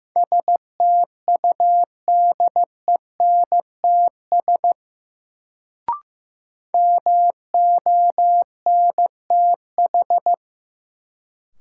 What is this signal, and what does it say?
Morse code, heard as a keyed tone, about 15 wpm, 700 Hz. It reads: STUDENTS MONTH
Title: Sets of 5 Words Encore - New Word Drill - 15wpm